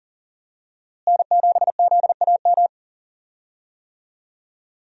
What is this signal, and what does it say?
Morse code, heard as a keyed tone, about 37 wpm, 700 Hz. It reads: N7ZAM
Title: USA Call Signs - 5 Characters - Repeated 3x - 40wpm